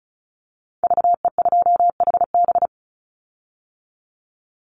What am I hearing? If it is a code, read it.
Morse code, heard as a keyed tone, about 34 wpm, 700 Hz. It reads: VE2HB
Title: Call Signs - 35wpm